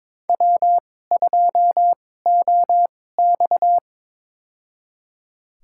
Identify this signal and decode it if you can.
Morse code, heard as a keyed tone, about 22 wpm, spaced as for 16 wpm, 700 Hz. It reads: W2OX
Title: USA Call Signs - 4 Characters - 2x Character Spacing - 22wpm